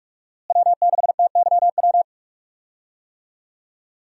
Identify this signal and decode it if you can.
Morse code, heard as a keyed tone, about 44 wpm, 700 Hz. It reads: W6TYW
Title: USA Call Signs - 5 Characters - Repeated 2x - 45wpm